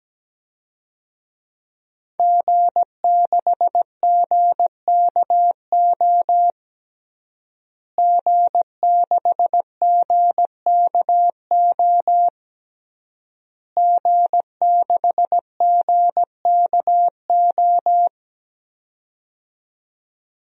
Morse code, heard as a keyed tone, about 17 wpm, 700 Hz. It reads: G6GKO G6GKO G6GKO
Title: DXCC-100 Call Signs - 5 Characters - Repeated 3x - 17wpm